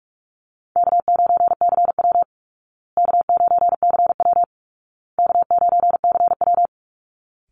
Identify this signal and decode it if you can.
Morse code, heard as a keyed tone, about 44 wpm, 700 Hz. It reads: K9CW K9CW K9CW
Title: Call Signs Repeated 2x - 45wpm